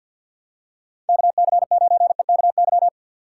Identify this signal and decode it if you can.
Morse code, heard as a keyed tone, about 53 wpm, 700 Hz. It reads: KC9EKY
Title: USA Call Signs - 6 Characters - 4x Character Spacing - 50wpm